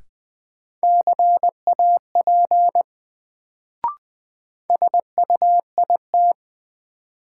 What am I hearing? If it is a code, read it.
Morse code, heard as a keyed tone, about 20 wpm, 700 Hz. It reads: CAP SUIT
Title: Sentences from Top 1600 Words - New Word Drill - 20wpm